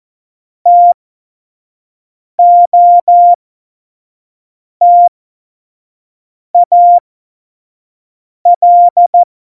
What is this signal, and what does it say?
Morse code, heard as a keyed tone, about 14 wpm, spaced as for 5 wpm, 700 Hz. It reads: TOTAL
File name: Morsekurs / Witze / Witz 28 Juli 2020, 25 F70